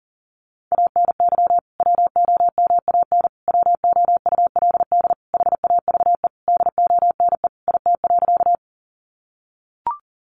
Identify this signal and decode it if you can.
Morse code, heard as a keyed tone, about 37 wpm, 700 Hz. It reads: ANY WOMAN WOULD HAVE DONE IT.
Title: Sentences from Top 1100 Words 40wpm